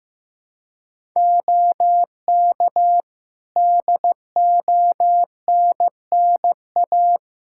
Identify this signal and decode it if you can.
Morse code, heard as a keyed tone, about 15 wpm, 700 Hz. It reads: OK DONNA
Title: Practice QSO - Exchange 4 - Ending - 15wpm